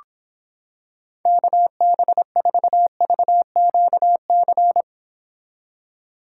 Morse code, heard as a keyed tone, about 26 wpm, 700 Hz. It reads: KB4VQC